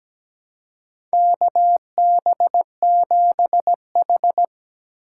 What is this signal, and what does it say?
Morse code, heard as a keyed tone, about 17 wpm, 700 Hz. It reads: KB7H
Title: K1USN SST QSO Practice - 17wpm